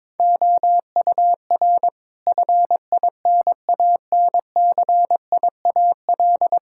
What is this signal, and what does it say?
Morse code, heard as a keyed tone, about 22 wpm, 700 Hz. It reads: OUR FINANCIAL
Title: Sentences from Top 1000 Words - Review Part 5 - 22wpm